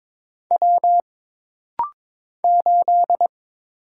Morse code, heard as a keyed tone, about 22 wpm, 700 Hz. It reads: W 8
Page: This rapid-fire timing is geared to push your ICR (Instant Character Recognition) skills.